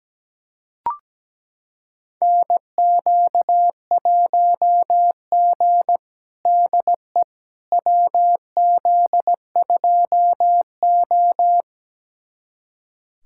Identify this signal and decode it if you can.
Morse code, heard as a keyed tone, about 17 wpm, 700 Hz. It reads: NQ1G DE WZ2O
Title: USA Callsign DE Callsign - 4 Characters - 17wpm